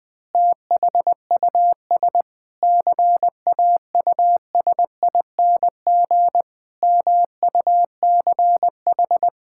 Morse code, heard as a keyed tone, about 20 wpm, 700 Hz. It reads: THUS CAUSING MUCH
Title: Pangram Challenge v2 - 20wpm